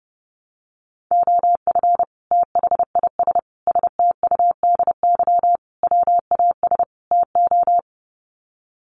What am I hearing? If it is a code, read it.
Morse code, heard as a keyed tone, about 30 wpm, 700 Hz. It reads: OF THIS STUDY WAS TO
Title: Sets of 5 Words 30wpm